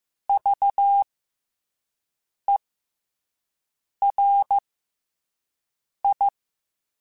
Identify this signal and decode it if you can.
Morse code, heard as a keyed tone, about 15 wpm, 800 Hz. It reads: V E R I
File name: Special INGs-mp3s / Special INGs-mp3s / SlowToFaster ing @10